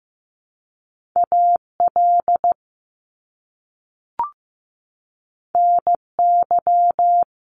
Morse code, heard as a keyed tone, about 15 wpm, 700 Hz. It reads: AL NY
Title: Top Two Letter Combinations in Words 15wpm